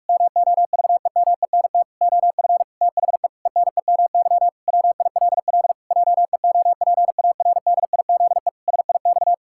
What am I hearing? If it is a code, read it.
Morse code, heard as a keyed tone, about 46 wpm, 700 Hz. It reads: MOVEMENT OF THE ENEMY WILL JEOPARDIZE SIX
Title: Pangram Challenge v2 - 45wpm